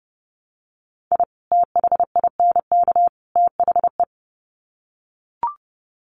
Morse code, heard as a keyed tone, about 30 wpm, 700 Hz. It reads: I THINK THE
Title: Sets of 3 Words 30wpm